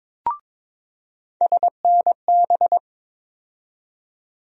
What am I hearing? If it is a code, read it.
Morse code, heard as a keyed tone, about 23 wpm, 700 Hz. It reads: SNB
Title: USA Call Sign Postfixes - 3 Letters 22wpm